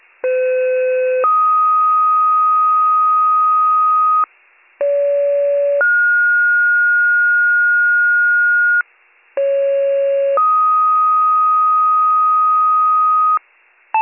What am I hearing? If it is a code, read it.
Two-tone page, heard as a one-second tone, then a three-second tone, about 510 then 1250 Hz, about 570 then 1430 Hz, about 550 then 1250 Hz. It is heard from 0.2 to 13.4 s.